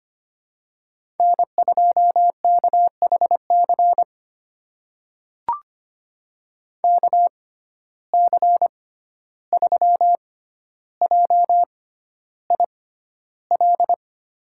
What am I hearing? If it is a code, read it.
Morse code, heard as a keyed tone, about 25 wpm, spaced as for 12 wpm, 700 Hz. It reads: N2KHC KC3JIL
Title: USA Call Signs - 6x Character Spacing - 25wpm